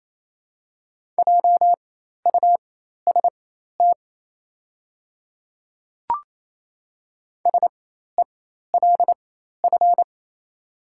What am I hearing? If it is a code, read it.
Morse code, heard as a keyed tone, about 28 wpm, spaced as for 13 wpm, 700 Hz. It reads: JUST SELF
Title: 4x Character Spacing - Top 200 Words - 28wpm